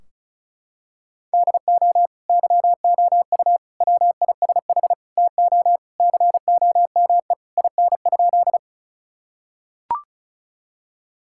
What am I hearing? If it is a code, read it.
Morse code, heard as a keyed tone, about 36 wpm, 700 Hz. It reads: DO YOU WISH TO COME IN?
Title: Sentences from Top 1600 Words 35wpm